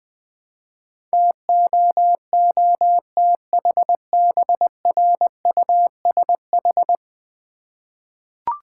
Morse code, heard as a keyed tone, about 21 wpm, 700 Hz. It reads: TOOTHBRUSH
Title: Easy Words with Hints - 20wpm